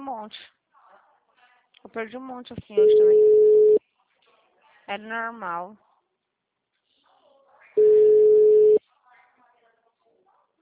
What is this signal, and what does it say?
Ringback tone (Europe, 425 Hz), with speech.
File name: out- -2013- - - .94630